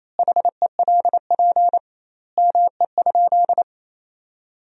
Morse code, heard as a keyed tone, about 28 wpm, 700 Hz. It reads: HELP ME?
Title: 2x Word-Spacing - Sentences from Top 200 Words - 28wpm